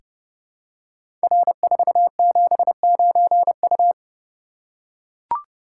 Morse code, heard as a keyed tone, about 30 wpm, 700 Hz. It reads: R479U
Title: Code Groups - 5 Letter:Numbers - 30wpm